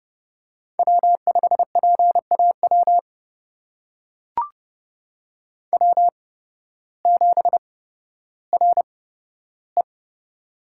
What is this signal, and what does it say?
Morse code, heard as a keyed tone, about 31 wpm, spaced as for 14 wpm, 700 Hz. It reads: W5PAW W7RE
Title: USA Call Signs - 8x Character Spacing - 30wpm